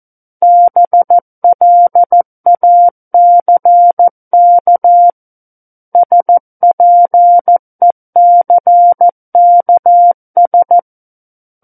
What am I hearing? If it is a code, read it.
Morse code, heard as a keyed tone, about 14 wpm, 700 Hz. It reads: BLACK SPECKS
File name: Book One / 05-Chapter 5 - 14 WPM